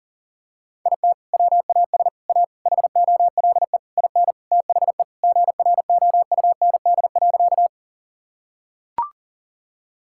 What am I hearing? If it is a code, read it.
Morse code, heard as a keyed tone, about 40 wpm, 700 Hz. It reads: IT WAS A HOLE IN THE GROUND.
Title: Sentences from Top 1600 Words 40wpm